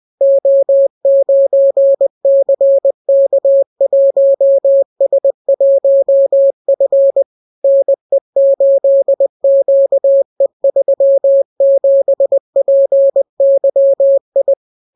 Morse code, heard as a keyed tone, about 20 wpm, 550 Hz. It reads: O9CK1S1F NE8QE37PYI